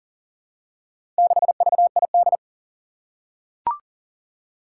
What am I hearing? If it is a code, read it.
Morse code, heard as a keyed tone, about 40 wpm, 700 Hz. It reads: 6VID
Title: Four Challenging Letter-Number Combinations 40wpm